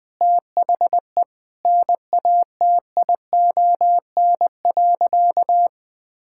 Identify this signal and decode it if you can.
Morse code, heard as a keyed tone, about 20 wpm, 700 Hz. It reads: THE NATION.
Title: Sentences with 7-Word Phrases - Spoken Beginning - 20wpm